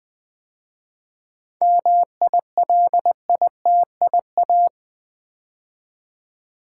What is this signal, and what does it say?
Morse code, heard as a keyed tone, about 20 wpm, 700 Hz. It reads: MILITIA
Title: 7 Letter Words - First 3 Letters Given - 20wpm